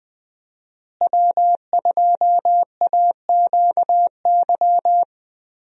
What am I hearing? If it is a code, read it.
Morse code, heard as a keyed tone, about 20 wpm, 700 Hz. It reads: W2AQY
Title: USA Call Signs - 5 Characters - Repeated 3x - 20wpm